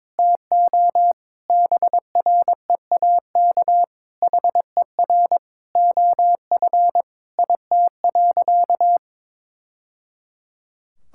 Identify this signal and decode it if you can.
Morse code, heard as a keyed tone, about 22 wpm, 700 Hz. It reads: TO BREAK HER OF IT.
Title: Sentences from Top 1000 Words - Review Part 6 - 22wpm